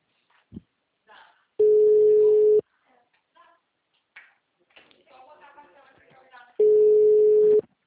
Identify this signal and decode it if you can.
Ringback tone (Europe, 425 Hz), with speech.
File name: out- -2012- - - .25260